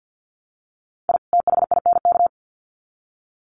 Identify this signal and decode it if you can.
Morse code, heard as a keyed tone, about 50 wpm, 700 Hz. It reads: I THINK
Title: Sets of 2 Words 50wpm